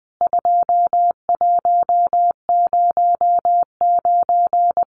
Morse code, heard as a keyed tone, about 20 wpm, 700 Hz. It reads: 2109
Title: CWT Exchange 20wpm